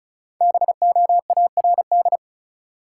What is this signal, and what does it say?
Morse code, heard as a keyed tone, about 34 wpm, 700 Hz. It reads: BOARD
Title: Five Letter Words - 35wpm